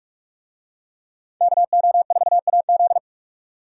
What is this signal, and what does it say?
Morse code, heard as a keyed tone, about 40 wpm, 700 Hz. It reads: KO4AZ